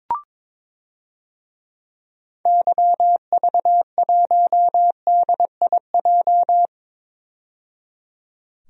Morse code, heard as a keyed tone, about 22 wpm, 700 Hz. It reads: YV1DIJ
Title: DXCC-100 Call Signs - 6 Characters - 22wpm